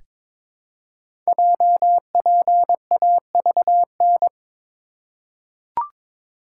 Morse code, heard as a keyed tone, about 23 wpm, 700 Hz. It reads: JPAVN
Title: Code Groups - 5 Letters - 22wpm